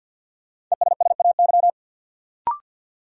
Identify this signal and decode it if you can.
Morse code, heard as a keyed tone, about 52 wpm, 700 Hz. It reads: ESSAY
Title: Sentences from Top 1500 Words - New Word Drill - 50wpm